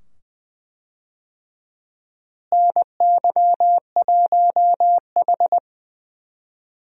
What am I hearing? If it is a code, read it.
Morse code, heard as a keyed tone, about 20 wpm, 700 Hz. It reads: NY1H
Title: USA Call Signs - 4 Characters - Repeated 2x - 20wpm